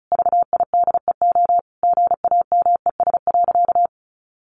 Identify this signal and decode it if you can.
Morse code, heard as a keyed tone, about 34 wpm, 700 Hz. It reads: VIDEO GAMES.
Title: Additional Easy Phrases 35wpm